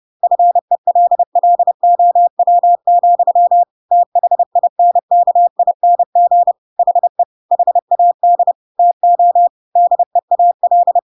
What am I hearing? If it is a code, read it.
Morse code, heard as a keyed tone, about 30 wpm, 700 Hz. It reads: FELLOW, THINKING HE HAD TO DEAL
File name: Pinocchio / Pinocchio ch6 30 30 ik1hge